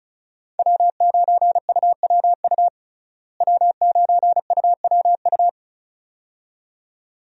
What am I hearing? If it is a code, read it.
Morse code, heard as a keyed tone, about 36 wpm, 700 Hz. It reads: W9UWU W9UWU
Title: USA Call Signs - 5 Characters - Repeated 2x - 35wpm